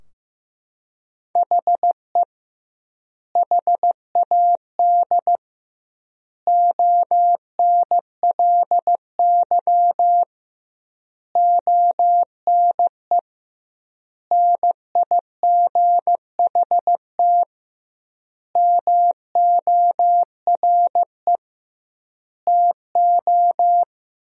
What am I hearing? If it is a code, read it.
Morse code, heard as a keyed tone, about 15 wpm, 700 Hz. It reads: HE HAD ONLY ONE NIGHT MORE TO
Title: 2x Word-Spacing - Sentences from Top 300 Words - 15wpm